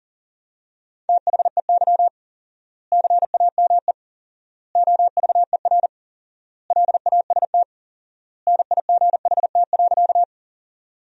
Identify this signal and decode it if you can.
Morse code, heard as a keyed tone, about 40 wpm, 700 Hz. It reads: THEY CAME OVER LAST NIGHT.
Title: 4x Word-Spacing - Sentences from Top 300 Words - 40wpm